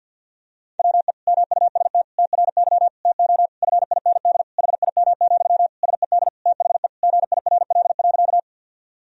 Morse code, heard as a keyed tone, about 46 wpm, 700 Hz. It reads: WE MUST TRY TO FIND HIM, SED THE GIRL.
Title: Sentences from Top 400 Words and Abbreviations 50wpm